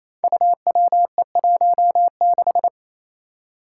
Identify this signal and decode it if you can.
Morse code, heard as a keyed tone, about 28 wpm, 700 Hz. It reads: UWE16
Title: Five Challenging Letter-Number Combinations 28wpm